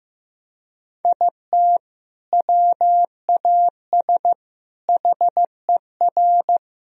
Morse code, heard as a keyed tone, about 15 wpm, 700 Hz. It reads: IT WAS HER
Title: Sentences from Top 1000 Words - Review Part 1 - 15wpm